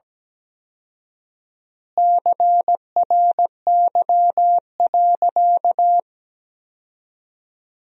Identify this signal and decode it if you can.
Morse code, heard as a keyed tone, about 17 wpm, 700 Hz. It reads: CRY.
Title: 4x Word-Spacing - Sentences from Top 500 Words - 17wpm